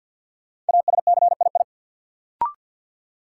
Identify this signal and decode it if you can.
Morse code, heard as a keyed tone, about 48 wpm, 700 Hz. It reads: ASCII
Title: Sets of 5 Words - New Word Drill - 50wpm